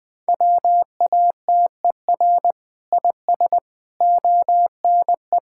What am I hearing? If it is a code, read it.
Morse code, heard as a keyed tone, about 19 wpm, 700 Hz. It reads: WATER IS ONE
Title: Sentences with 5-Word Phrases - 20wpm